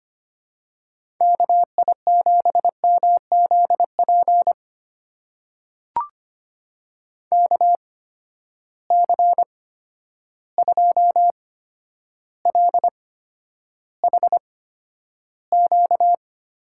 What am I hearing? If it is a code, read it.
Morse code, heard as a keyed tone, about 25 wpm, spaced as for 11 wpm, 700 Hz. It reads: KI7MZP KC2LHQ
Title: USA Call Signs - 6 Characters - 8x Character Spacing - 25wpm